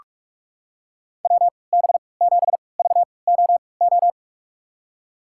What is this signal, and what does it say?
Morse code, heard as a keyed tone, about 45 wpm, spaced as for 25 wpm, 700 Hz. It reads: WB7VOO
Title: USA Call Signs - 6 Characters - 3x Character Spacing - 45wpm